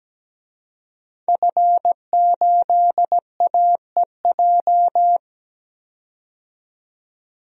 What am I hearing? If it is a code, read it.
Morse code, heard as a keyed tone, about 17 wpm, 700 Hz. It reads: F8AEJ
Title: DXCC-100 Call Signs - 5 Characters - 17wpm